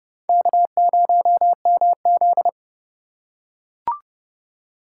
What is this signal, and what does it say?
Morse code, heard as a keyed tone, about 28 wpm, 700 Hz. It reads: K0MZ